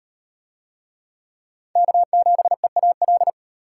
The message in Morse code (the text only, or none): K7EAL